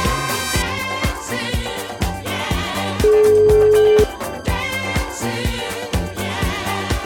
Ringback tone (Europe, 425 Hz), with noise.